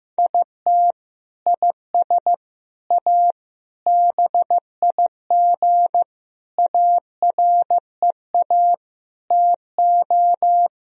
Morse code, heard as a keyed tone, about 15 wpm, 700 Hz. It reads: IT IS A BIG AREA TO
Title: Sentences from Top 1000 Words - Review Part 2 - 15wpm